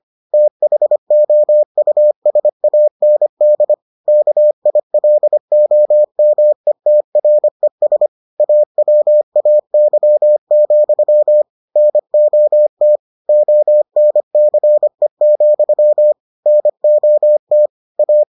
Morse code, heard as a keyed tone, about 25 wpm, 600 Hz. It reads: THOUSAND KILOMETRES AWAY, NOT ONCE, NOT A